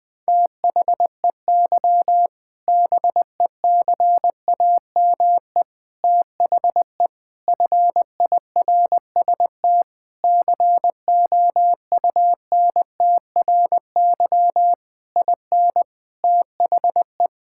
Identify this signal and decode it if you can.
Morse code, heard as a keyed tone, about 20 wpm, 700 Hz. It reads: THEY BECAME THE FIRST COUNTRY IN THE
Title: Sentences with 8-Word Phrases - 20wpm